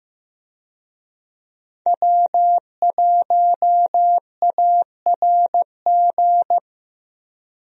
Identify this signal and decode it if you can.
Morse code, heard as a keyed tone, about 15 wpm, 700 Hz. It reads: W1ARG
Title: USA Call Signs - 5 Characters - Repeated 2x - 15wpm